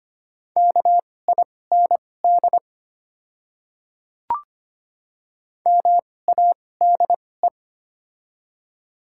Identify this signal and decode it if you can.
Morse code, heard as a keyed tone, about 25 wpm, spaced as for 18 wpm, 700 Hz. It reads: KIND MADE
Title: 2x Character Spacing - Top 300 Words - 25wpm